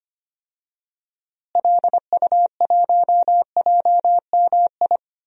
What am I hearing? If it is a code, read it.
Morse code, heard as a keyed tone, about 25 wpm, 700 Hz. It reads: LU1JMI
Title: DXCC-100 Call Signs - 6 Characters - Repeated 2x - 25wpm